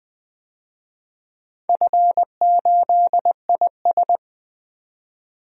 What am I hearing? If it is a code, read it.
Morse code, heard as a keyed tone, about 20 wpm, 700 Hz. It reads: F8IS
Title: DXCC-100 Call Signs - 4 Characters - 20wpm